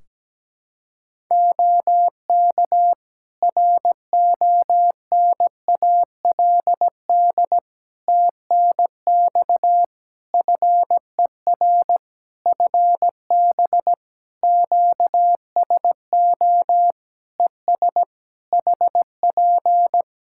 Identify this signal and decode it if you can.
Morse code, heard as a keyed tone, about 17 wpm, 700 Hz. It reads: OK RONALD TNX FER FB QSO ES HP
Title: Practice QSO - Exchange 4 - Ending - 17wpm